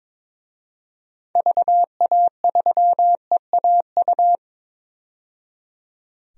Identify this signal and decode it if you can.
Morse code, heard as a keyed tone, about 22 wpm, 700 Hz. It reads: VA3EAU
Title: DXCC-100 Call Signs - 6 Characters - 22wpm